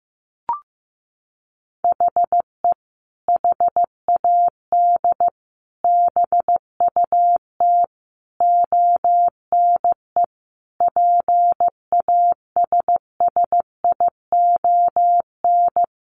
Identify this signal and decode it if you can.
Morse code, heard as a keyed tone, about 15 wpm, 700 Hz. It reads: HE HAD BUT ONE PASSION
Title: Sentences from Top 1300 Words 15wpm